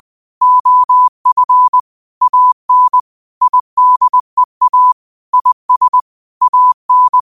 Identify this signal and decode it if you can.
Morse code, heard as a keyed tone, about 20 wpm, 1000 Hz. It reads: OF AN IDEA IS AN